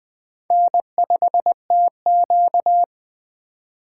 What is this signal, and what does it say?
Morse code, heard as a keyed tone, about 20 wpm, 700 Hz. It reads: N5TQ
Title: Speed Racing - USA 1x2 Call Signs 17wpm